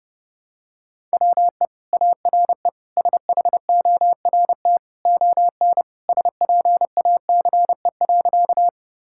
Morse code, heard as a keyed tone, about 31 wpm, 700 Hz. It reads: WE ARE SHORT ON SPACE.
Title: Sentences from Top 1000 Words - Review Part 5 - 30wpm